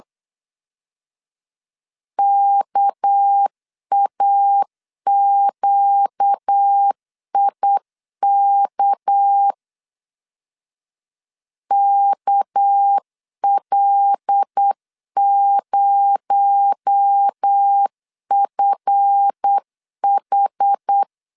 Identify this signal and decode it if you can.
Morse code, heard as a keyed tone, about 8 wpm, 800 Hz. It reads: KAQIK KL0FH
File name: LEZIONI / 7 GF9Z / 7 GF 8wpm 3108